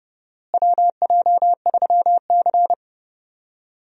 Morse code, heard as a keyed tone, about 31 wpm, 700 Hz. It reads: WJ3C
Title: USA Call Signs - 4 Characters - Repeated 2x - 30wpm